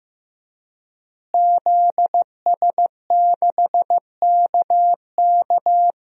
Morse code, heard as a keyed tone, about 15 wpm, 700 Hz. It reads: ZS6KK
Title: DXCC-100 Call Signs - 5 Characters - Repeated 3x - 15wpm